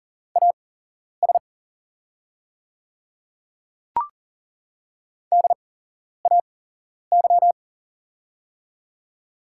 Morse code, heard as a keyed tone, about 40 wpm, spaced as for 11 wpm, 690 Hz. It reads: AS DAY
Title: 8x Character Spacing - Top 100 Words - 40wpm